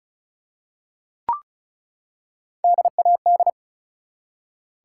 Morse code, heard as a keyed tone, about 36 wpm, 700 Hz. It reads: DAD
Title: Three Letter Words - 35wpm